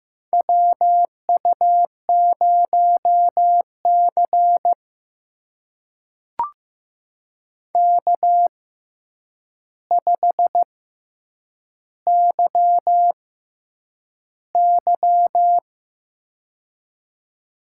Morse code, heard as a keyed tone, about 15 wpm, spaced as for 8 wpm, 700 Hz. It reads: WU0C K5YY
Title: USA Call Signs - 4 Characters - 6x Character Spacing - 15wpm